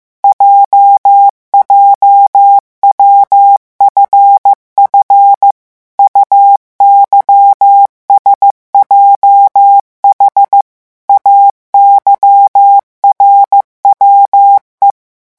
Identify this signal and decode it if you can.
Morse code, heard as a keyed tone, about 15 wpm, 800 Hz. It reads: JJWFF UYSJH AYRWE
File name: morse mp3 / revision7-vitesse 14 mots